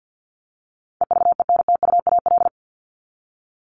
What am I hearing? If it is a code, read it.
Morse code, heard as a keyed tone, about 50 wpm, 700 Hz. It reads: EVENTUAL
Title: LY Suffix - Top 100 Words with Base Words 50wpm